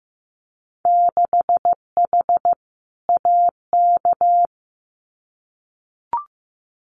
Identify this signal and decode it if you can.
Morse code, heard as a keyed tone, about 15 wpm, 700 Hz. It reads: 6H AK
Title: Winter Field Day Exchanges Encore 15wpm